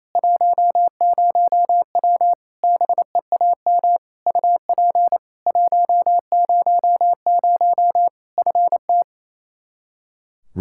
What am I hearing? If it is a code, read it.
Morse code, heard as a keyed tone, about 28 wpm, 700 Hz. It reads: 10W BEAM UP 100 FT
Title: Practice QSO - Exchange 2 - Info - 28wpm